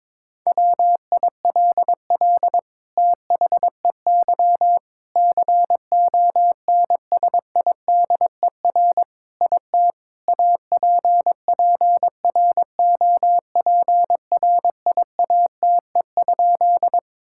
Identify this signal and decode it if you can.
Morse code, heard as a keyed tone, about 22 wpm, 700 Hz. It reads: WILL THEY CONSIDER IT APPROPRIATE?
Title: Sentences from Top 1000 Words - Review Part 5 - 22wpm